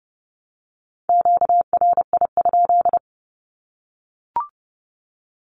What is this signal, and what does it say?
Morse code, heard as a keyed tone, about 30 wpm, 700 Hz. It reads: QRI?
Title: Q-Codes Repeated 3x 30wpm